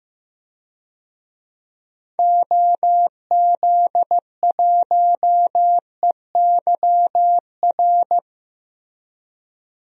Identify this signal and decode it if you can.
Morse code, heard as a keyed tone, about 16 wpm, 700 Hz. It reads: OZ1EYR
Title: DXCC-100 Call Signs - 6 Characters - Repeated 2x - 15wpm